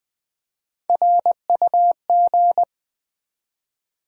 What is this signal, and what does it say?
Morse code, heard as a keyed tone, about 20 wpm, 700 Hz. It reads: RUG
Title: Build A Word - 1 Letter At A Time - 3 to 5-Letter Words - 20wpm